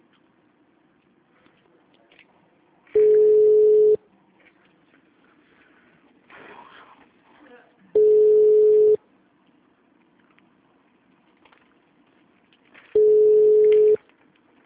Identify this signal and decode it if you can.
Ringback tone (Europe, 425 Hz), with speech.